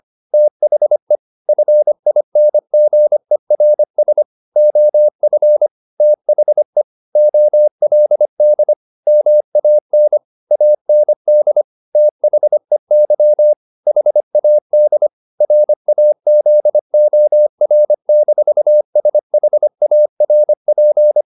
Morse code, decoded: THE FINGERS OF THE OLD MAN AND THEY HAD RAZOR-SHARP

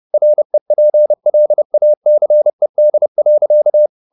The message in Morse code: REPLACED.

